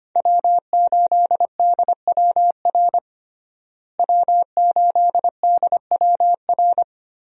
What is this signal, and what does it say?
Morse code, heard as a keyed tone, about 25 wpm, 700 Hz. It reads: W8DWR W8DWR
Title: USA Call Signs - 5 Characters - Repeated 3x - 25wpm